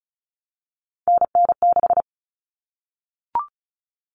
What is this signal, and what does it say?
Morse code, heard as a keyed tone, about 34 wpm, 700 Hz. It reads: NN6
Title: Common Cut Numbers - 3-Digit 35wpm